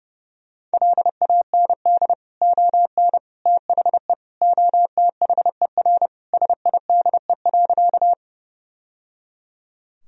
Morse code, heard as a keyed tone, about 30 wpm, 700 Hz. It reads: LAND ON THE OTHER SIDE.
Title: Sentences from Top 1000 Words - Review Part 1 - 30wpm